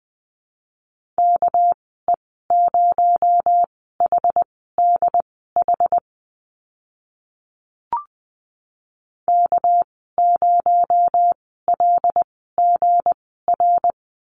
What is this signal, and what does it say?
Morse code, heard as a keyed tone, about 20 wpm, spaced as for 15 wpm, 700 Hz. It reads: KE0HDH K0LGR
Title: 231 Lesson 37g - Letter x - Call Signs 20wpm